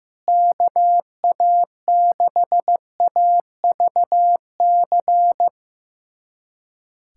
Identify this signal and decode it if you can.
Morse code, heard as a keyed tone, about 15 wpm, 700 Hz. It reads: KA6AVC
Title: USA Callsign DE Callsign - 6 Characters - 15wpm